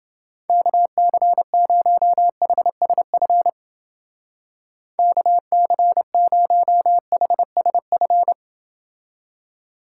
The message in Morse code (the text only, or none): KC0HSF KC0HSF